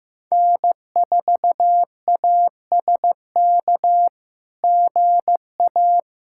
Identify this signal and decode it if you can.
Morse code, heard as a keyed tone, about 15 wpm, 700 Hz. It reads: N4ASK GA
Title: SOTA QSO Practice 15wpm